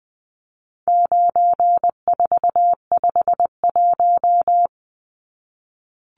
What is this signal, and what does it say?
Morse code, heard as a keyed tone, about 20 wpm, 700 Hz. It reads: 9451
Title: Numbers - 4-Digit 20wpm